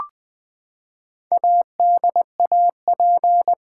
Morse code, heard as a keyed tone, about 20 wpm, 700 Hz. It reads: ADAP
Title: Build A Word - 4 Letters At A Time - 10-Letter Words - 20wpm